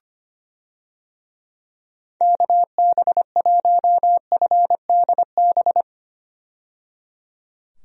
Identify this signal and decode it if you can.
Morse code, heard as a keyed tone, about 25 wpm, 700 Hz. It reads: KB1FDB
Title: USA Call Signs - 6 Characters - 25wpm